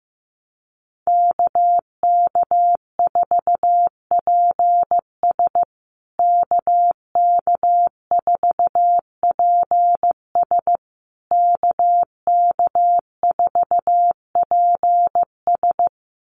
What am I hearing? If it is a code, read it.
Morse code, heard as a keyed tone, about 15 wpm, 700 Hz. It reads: KK4PS KK4PS KK4PS
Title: USA Call Signs Repeated 3x 15wpm